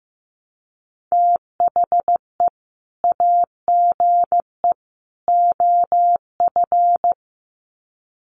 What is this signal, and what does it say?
Morse code, heard as a keyed tone, about 15 wpm, 700 Hz. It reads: THE AGE OF
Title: Sets of 3 Words 15wpm